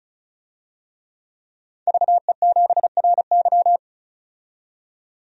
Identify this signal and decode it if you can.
Morse code, heard as a keyed tone, about 34 wpm, 700 Hz. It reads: VE7RY
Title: DXCC-100 Call Signs - 5 Characters - 35wpm